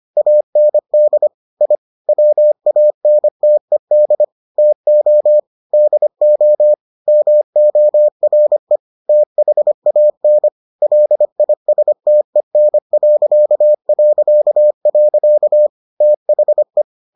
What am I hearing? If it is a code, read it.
Morse code, heard as a keyed tone, about 25 wpm, 600 Hz. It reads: AND I WANTED TO DO MORE THAN LISTEN... THE